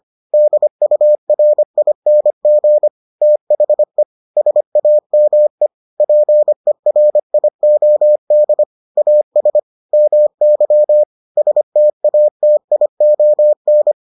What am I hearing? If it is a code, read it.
Morse code, heard as a keyed tone, about 25 wpm, 600 Hz. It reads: DURING THE SAME PERIOD AS MY STATION